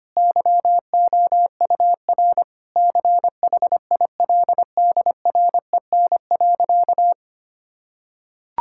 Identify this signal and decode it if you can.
Morse code, heard as a keyed tone, about 25 wpm, 700 Hz. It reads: YOUR CHILDREN.
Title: Sentences from Top 1000 Words - Review Part 6 - 25wpm